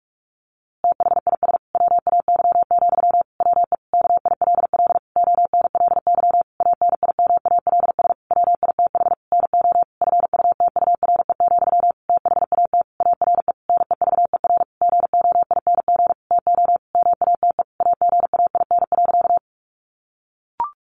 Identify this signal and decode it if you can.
Morse code, heard as a keyed tone, about 44 wpm, 700 Hz. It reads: THIS WAY, WE KILL ONLY ANIMALS WITH NO FUTURE, THAT ARE NEVER GOING TO MATE AGAIN.
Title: A Sound Of Thunder 45wpm